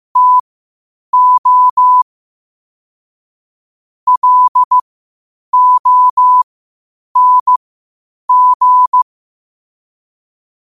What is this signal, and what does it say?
Morse code, heard as a keyed tone, about 15 wpm, spaced as for 8 wpm, 1000 Hz. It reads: TO LONG
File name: qotdfiles / 8wpm